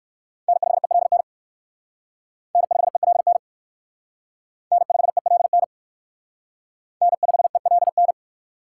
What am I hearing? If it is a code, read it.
Morse code, heard as a keyed tone, about 54 wpm, 700 Hz. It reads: N5ELN N5ELN N5ELN N5ELN
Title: Speed Racing - USA 1x3 Call Signs 45wpm